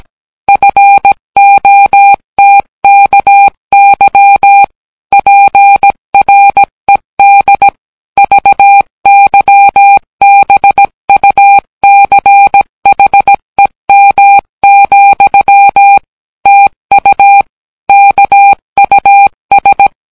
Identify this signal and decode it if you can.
Morse code, heard as a keyed tone, about 17 wpm, 800 Hz. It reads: FOTKY PRED VYBUCHEM, TU KUS